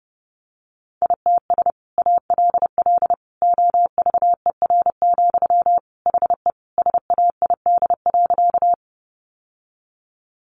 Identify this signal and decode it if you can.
Morse code, heard as a keyed tone, about 30 wpm, 700 Hz. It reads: ITS ALL OVER, HE SAID.
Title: Sentences from Top 600 Words 30wpm